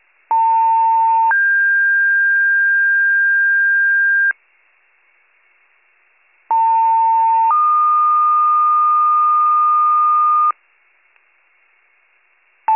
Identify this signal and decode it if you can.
Two-tone page, heard as a one-second tone, then a three-second tone, about 900 then 1600 Hz, about 900 then 1220 Hz. It is heard from 0.3 to 10.5 s.